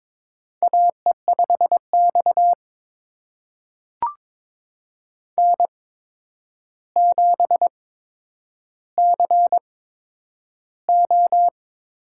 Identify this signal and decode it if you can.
Morse code, heard as a keyed tone, about 22 wpm, spaced as for 9 wpm, 700 Hz. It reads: AE5X N7CO
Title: USA Call Signs - 4 Characters - 8x Character Spacing - 22wpm